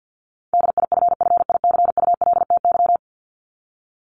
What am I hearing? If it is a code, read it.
Morse code, heard as a keyed tone, about 50 wpm, 700 Hz. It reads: DIFFICULTY
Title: Top 1251-1500 Words 50wpm